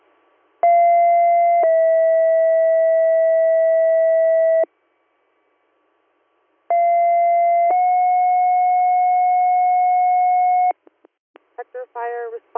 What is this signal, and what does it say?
Two-tone page, heard as a one-second tone, then a three-second tone, about 690 then 650 Hz, about 690 then 730 Hz, with speech. It is heard from 0.6 to 10.7 s.